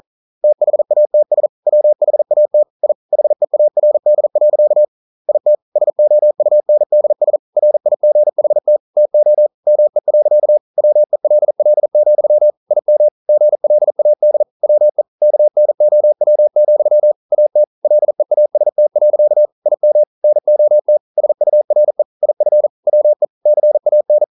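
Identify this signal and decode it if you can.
Morse code, heard as a keyed tone, about 40 wpm, 600 Hz. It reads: THATS WHAT I HEARD. IT SOUNDS RIGHT TO ME. WELL, IM GLAD WE KNOW, AT LEAST. IM NOT SURE IF WE CAN
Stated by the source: Morse code practice at 41 wpm, from CW med Gnister.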